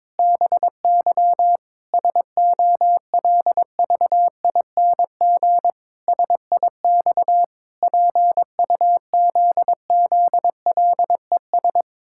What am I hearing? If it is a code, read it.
Morse code, heard as a keyed tone, about 22 wpm, 700 Hz. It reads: BY SOLVING SIX PUZZLES
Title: Pangram Challenge v2 - 22wpm